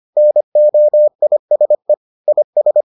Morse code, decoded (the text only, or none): NOISE IS